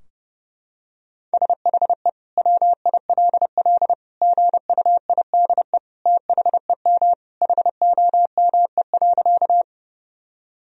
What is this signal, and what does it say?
Morse code, heard as a keyed tone, about 30 wpm, 700 Hz. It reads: SHE WILL GUIDE THEM HOME.
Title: Sentences from Top 1000 Words - Review Part 3 - 30wpm